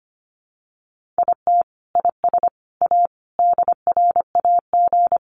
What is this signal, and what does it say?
Morse code, heard as a keyed tone, about 25 wpm, 700 Hz. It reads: IT IS A DRAG
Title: Sentences from Top 1500 Words 25wpm